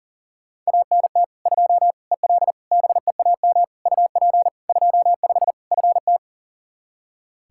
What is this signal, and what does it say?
Morse code, heard as a keyed tone, about 40 wpm, 700 Hz. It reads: ANT 2 EL BEAM UP 25 FT
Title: Practice QSO - Exchange 2 - Info - 40wpm